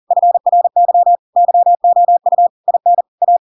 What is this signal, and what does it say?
Morse code, heard as a keyed tone, about 40 wpm, 700 Hz. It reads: FRY YOU IN A
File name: Pinocchio / Pinocchio ch5 40 40 ik1hge